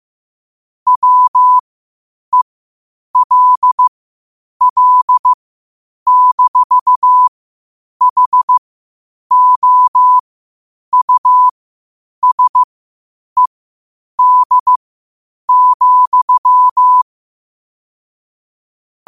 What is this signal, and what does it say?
Morse code, heard as a keyed tone, about 15 wpm, spaced as for 8 wpm, 1000 Hz. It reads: WELL-HOUSED,
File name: qotdfiles / 8wpm